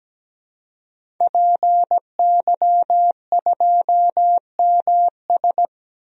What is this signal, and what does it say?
Morse code, heard as a keyed tone, about 17 wpm, 700 Hz. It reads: PY2MS
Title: DXCC-100 Call Signs - 5 Characters - 17wpm